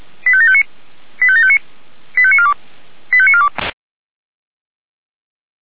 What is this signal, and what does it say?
Five-tone selective call (ZVEI), heard as quick runs of five tones, tones of about 70 ms. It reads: 85759 85759 85942 85942